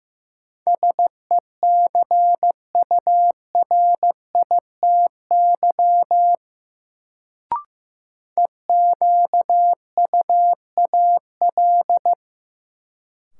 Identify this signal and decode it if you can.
Morse code, heard as a keyed tone, about 15 wpm, 700 Hz. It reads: SECURITY EQUAL
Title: Sentences from Top 1400 Words - New Word Drill Alt - 15wpm